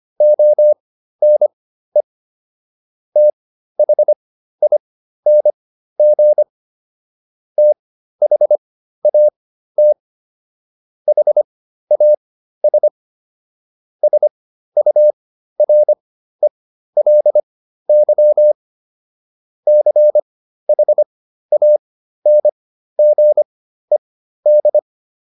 Morse code, heard as a keyed tone, about 25 wpm, spaced as for 13 wpm, 600 Hz. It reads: ONE THING THAT HAS SURELY CHANGED